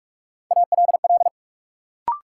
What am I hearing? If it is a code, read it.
Morse code, heard as a keyed tone, about 46 wpm, 700 Hz. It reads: ALL
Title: Sets of 2 Words - New Word Drill - 45wpm